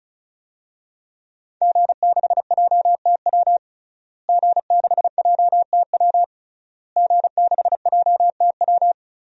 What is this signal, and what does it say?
Morse code, heard as a keyed tone, about 34 wpm, 700 Hz. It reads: G6JTW G6JTW G6JTW
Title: DXCC-100 Call Signs - 5 Characters - Repeated 3x - 35wpm